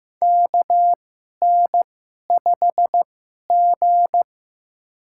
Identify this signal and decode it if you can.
Morse code, heard as a keyed tone, about 15 wpm, spaced as for 11 wpm, 700 Hz. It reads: KN5G
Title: USA Call Signs - 4 Characters - 2x Character Spacing - 15wpm